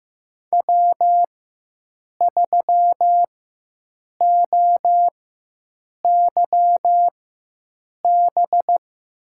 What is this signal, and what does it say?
Morse code, heard as a keyed tone, about 15 wpm, spaced as for 7 wpm, 700 Hz. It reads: W3OYB
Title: USA Call Signs - 5 Characters - 4x Character Spacing - 15wpm